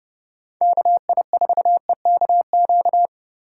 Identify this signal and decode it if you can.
Morse code, heard as a keyed tone, about 30 wpm, 700 Hz. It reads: KI4EKQ
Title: USA Call Signs - 4x Character Spacing - 30wpm